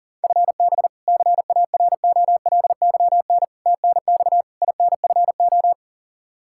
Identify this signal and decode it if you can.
Morse code, heard as a keyed tone, about 37 wpm, 690 Hz. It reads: FB CAROLYN TNX INFO
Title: Practice QSO - Exchange 3 - More Info - 40wpm